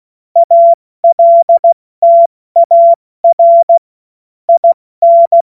Morse code, heard as a keyed tone, about 16 wpm, 680 Hz. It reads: ALTAR IN
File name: Morsekurs / Witze / Witz 06. September 2022 70 F80